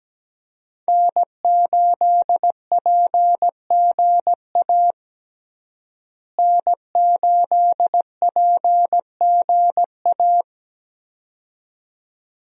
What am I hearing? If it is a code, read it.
Morse code, heard as a keyed tone, about 17 wpm, 700 Hz. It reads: N8PGA N8PGA
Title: USA Call Signs - 5 Characters - Repeated 2x - 17wpm